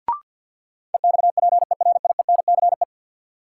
Morse code, heard as a keyed tone, about 50 wpm, 700 Hz. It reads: EXPERIENCE
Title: Sentences from Top 600 Words - New Word Drill - 50wpm